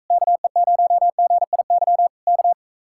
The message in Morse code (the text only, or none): KE0GIY K